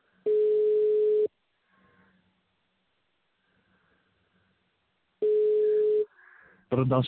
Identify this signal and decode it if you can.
Ringback tone (Europe, 425 Hz), with speech.